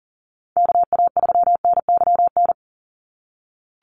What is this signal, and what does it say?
Morse code, heard as a keyed tone, about 40 wpm, 700 Hz. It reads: KA3NYN
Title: USA Call Signs 40wpm